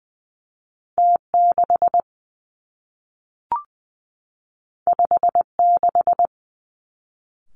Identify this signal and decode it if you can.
Morse code, heard as a keyed tone, about 20 wpm, 700 Hz. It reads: T 6 5 6
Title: Two Challenging Letter-Number Combinations 20wpm